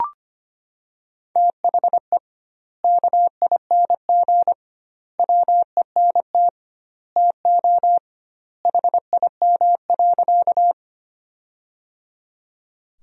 Morse code, heard as a keyed tone, about 25 wpm, 700 Hz. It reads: THE KING WENT TO HIM.
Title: 2x Word-Spacing - Sentences from Top 400 Words - 25wpm